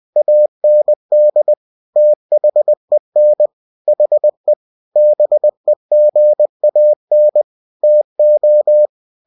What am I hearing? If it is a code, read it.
Morse code, heard as a keyed tone, about 20 wpm, 600 Hz. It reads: AND THEN HE BEGAN TO
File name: cw-books / winnie-the-pooh / winnie-20wpm-page-0039